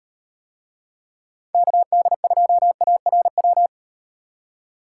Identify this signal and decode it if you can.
Morse code, heard as a keyed tone, about 38 wpm, 700 Hz. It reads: KD2ARW